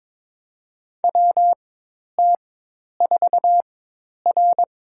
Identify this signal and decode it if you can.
Morse code, heard as a keyed tone, about 22 wpm, spaced as for 10 wpm, 700 Hz. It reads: WT4R
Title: USA Call Signs - 4 Characters - 4x Character Spacing - 22wpm